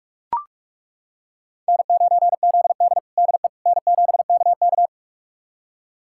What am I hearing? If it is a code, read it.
Morse code, heard as a keyed tone, about 41 wpm, 700 Hz. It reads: N9ZD DE N7KK
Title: USA Callsign DE Callsign - 4 Characters - 45wpm